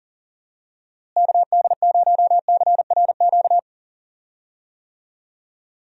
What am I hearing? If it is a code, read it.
Morse code, heard as a keyed tone, about 39 wpm, 700 Hz. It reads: KD0CRQ
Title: USA Call Signs - 6 Characters - Repeated 3x - 40wpm